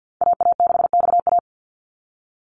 Morse code, heard as a keyed tone, about 50 wpm, 700 Hz. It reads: AA6XA
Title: Call Signs Repeated 2x - 50wpm